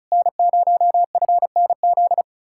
Morse code, heard as a keyed tone, about 43 wpm, 700 Hz. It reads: N0FNZ
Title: Speed-Racing Common QSO Phrases - 35wpm